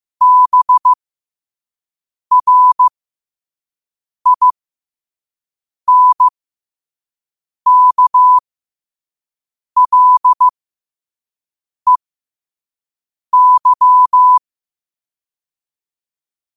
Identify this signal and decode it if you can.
Morse code, heard as a keyed tone, about 15 wpm, spaced as for 5 wpm, 1000 Hz. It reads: BRINKLEY
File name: qotdfiles / 5wpm